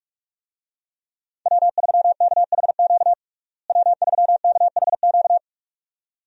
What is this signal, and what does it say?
Morse code, heard as a keyed tone, about 46 wpm, 700 Hz. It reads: W3KHQ W3KHQ
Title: USA Call Signs - 5 Characters - Repeated 2x - 45wpm